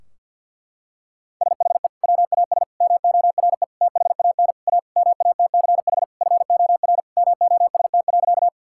Morse code, heard as a keyed tone, about 50 wpm, 700 Hz. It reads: SHE WAS MORE THAN A MATCH FOR MOST.
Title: Sentences from Top 1600 Words 50wpm